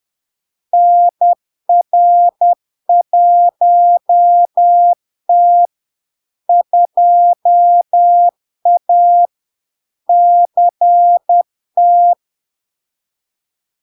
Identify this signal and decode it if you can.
Morse code, heard as a keyed tone, about 10 wpm, 700 Hz. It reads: NR1T 2A CT
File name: LARK-FD-2022-Log-in-CW-10wpm-700-Hz